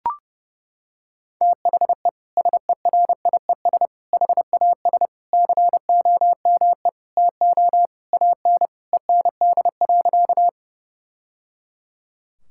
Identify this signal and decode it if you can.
Morse code, heard as a keyed tone, about 32 wpm, 700 Hz. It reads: THE SERIES HAS COME TO AN END.
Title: Sentences from Top 1000 Words - Review Part 5 - 30wpm